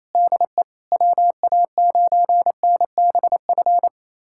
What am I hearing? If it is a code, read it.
Morse code, heard as a keyed tone, about 28 wpm, 700 Hz. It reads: DE WA9NBF
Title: USA Callsign DE Callsign 28wpm